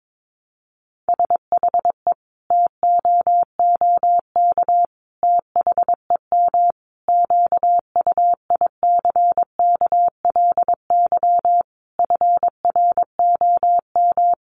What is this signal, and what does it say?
Morse code, heard as a keyed tone, about 22 wpm, 700 Hz. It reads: SHE TOOK THEM QUICKLY FROM
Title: Sentences from Top 1300 Words 22wpm